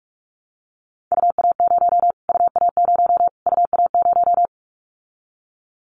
Morse code, heard as a keyed tone, about 44 wpm, 700 Hz. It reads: UA0 UA0 UA0
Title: Callsign Prefixes for Top 25 DXCC - Repeated 3x - 45wpm